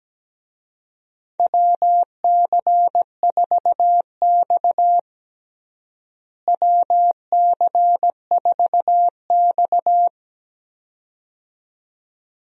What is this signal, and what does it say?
Morse code, heard as a keyed tone, about 17 wpm, 700 Hz. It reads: WC4X WC4X
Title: USA Call Signs - 4 Characters - Repeated 2x - 17wpm